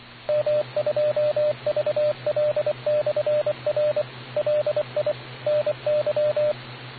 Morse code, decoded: M2VL/R LI NY